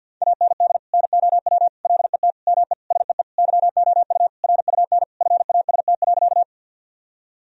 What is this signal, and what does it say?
Morse code, heard as a keyed tone, about 51 wpm, 700 Hz. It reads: AND NOW LET ME SEE YOU RUN FAST.
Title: Sentences from Top 1000 Words - Review Part 2 - 50wpm